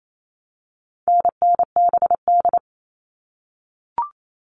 Morse code, heard as a keyed tone, about 28 wpm, 700 Hz. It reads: NNBD
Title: All Cut Numbers - 4-Digit 28wpm